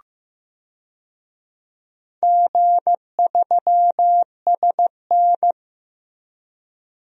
Morse code, heard as a keyed tone, about 15 wpm, 700 Hz. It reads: G3SN
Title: DXCC-100 Call Signs - 4 Characters - Repeated 2x - 15wpm